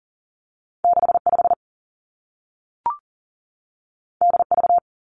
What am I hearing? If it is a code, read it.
Morse code, heard as a keyed tone, about 40 wpm, 700 Hz. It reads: 6 5 D V
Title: Two Challenging Letter-Number Combinations 40wpm